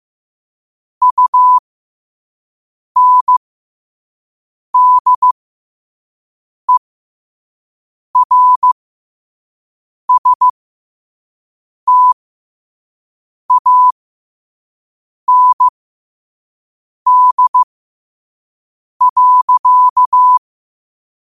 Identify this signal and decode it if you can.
Morse code, heard as a keyed tone, about 15 wpm, spaced as for 5 wpm, 1000 Hz. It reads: UNDERSTAND.
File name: qotdfiles / 5wpm